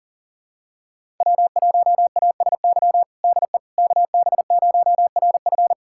Morse code, heard as a keyed tone, about 40 wpm, 700 Hz. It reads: W1ASY DE KB0RF
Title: USA Callsign DE Callsign - 5 Characters - 40wpm